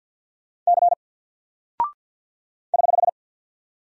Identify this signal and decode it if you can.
Morse code, heard as a keyed tone, about 50 wpm, 700 Hz. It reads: C <HH>
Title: Single Character (Letter:Number:Punctuation:Prosign) Alt 50wpm